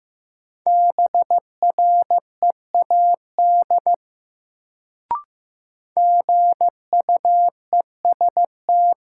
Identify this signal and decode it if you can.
Morse code, heard as a keyed tone, about 15 wpm, 700 Hz. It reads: BREAD GUEST
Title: Sentences from Top 1600 Words - New Word Drill - 15wpm